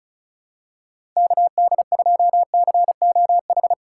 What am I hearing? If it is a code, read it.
Morse code, heard as a keyed tone, about 36 wpm, 700 Hz. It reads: KD2COH
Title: USA Call Signs - 6 Characters - 2x Character Spacing - 35wpm